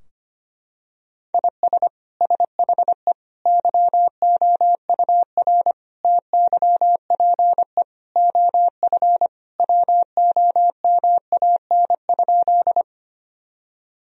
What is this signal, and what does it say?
Morse code, heard as a keyed tone, about 25 wpm, 700 Hz. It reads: IS SHE YOUR TYPE OF WOMAN?
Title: Sentences from Top 1000 Words - Review Part 3 - 25wpm